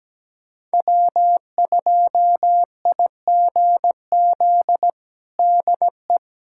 Morse code, heard as a keyed tone, about 17 wpm, 700 Hz. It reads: W2IGZ DE
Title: USA Callsign DE Callsign - 5 Characters - 17wpm